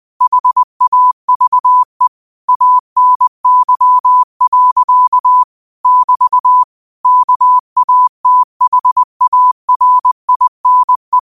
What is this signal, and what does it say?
Morse code, heard as a keyed tone, about 20 wpm, 1000 Hz. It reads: HAVE ANY. <BT> KATHARINE